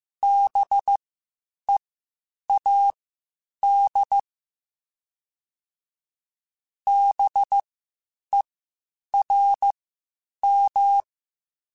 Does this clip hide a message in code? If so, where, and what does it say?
Yes: Morse code from 0.2 to 11.0 s, heard as a keyed tone, about 15 wpm, spaced as for 8 wpm, 800 Hz. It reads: BEAD BERM